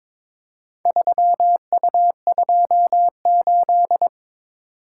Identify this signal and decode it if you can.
Morse code, heard as a keyed tone, about 22 wpm, 700 Hz. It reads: 3U28
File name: practice / 22wpm / Four Letter-Number Combinations - 22wpm